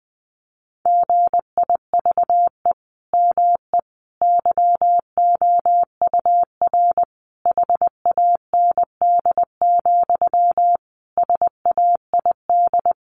Morse code, heard as a keyed tone, about 20 wpm, 700 Hz. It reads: GIVE ME YOUR HAND, SAID
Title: Sentences from Top 1500 Words 20wpm